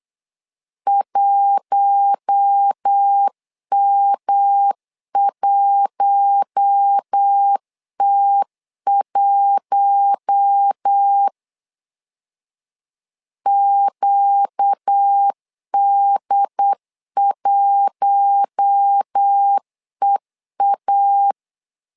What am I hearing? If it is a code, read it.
Morse code, heard as a keyed tone, about 8 wpm, 800 Hz. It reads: 1M1T1 QD1EA
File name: LEZIONI / 4 Q1 8wpm 3035